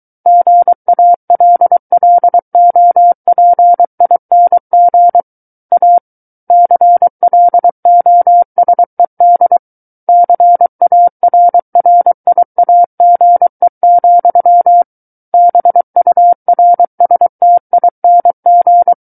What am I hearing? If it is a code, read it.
Morse code, heard as a keyed tone, about 23 wpm, 700 Hz. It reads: GALLOPING A CLOSED CARRIAGE, BURSTING
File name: Book One / 14-Chapter 14 - 23 WPM